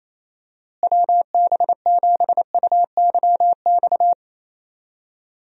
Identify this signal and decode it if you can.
Morse code, heard as a keyed tone, about 28 wpm, 700 Hz. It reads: WB7UYX
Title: USA Call Signs - 6 Characters - Repeated 3x - 28wpm